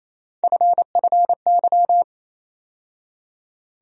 Morse code, heard as a keyed tone, about 27 wpm, 700 Hz. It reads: FFY
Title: USA Call Sign Postfixes - 3 Letters 28wpm